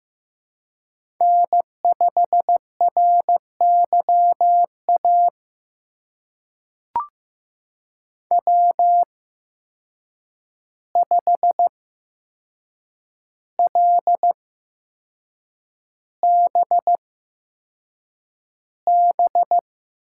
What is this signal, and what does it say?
Morse code, heard as a keyed tone, about 15 wpm, spaced as for 6 wpm, 700 Hz. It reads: N5RYA W5LBB
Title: USA Call Signs - 5 Characters - 8x Character Spacing - 15wpm